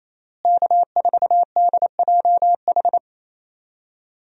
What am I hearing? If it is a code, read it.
Morse code, heard as a keyed tone, about 28 wpm, 700 Hz. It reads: K4DJH
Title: USA Call Signs - 5 Characters - Repeated 2x - 28wpm